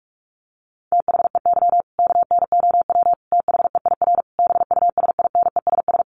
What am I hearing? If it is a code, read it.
Morse code, heard as a keyed tone, about 44 wpm, 700 Hz. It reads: THEY KNOW THEIR BUSINESS
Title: Sentences from Top 1200 Words 45wpm